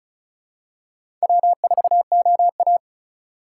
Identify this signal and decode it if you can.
Morse code, heard as a keyed tone, about 36 wpm, 700 Hz. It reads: W4OA
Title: USA Call Signs - 4 Characters - 4x Character Spacing - 35wpm